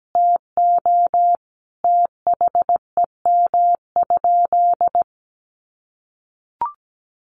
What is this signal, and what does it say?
Morse code, heard as a keyed tone, about 17 wpm, 700 Hz. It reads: TO THEM?
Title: Sentences from Top 1400 Words 17wpm